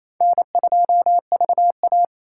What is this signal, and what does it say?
Morse code, heard as a keyed tone, about 35 wpm, 700 Hz. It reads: N2VA
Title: Speed Racing - USA 1x2 Call Signs 28wpm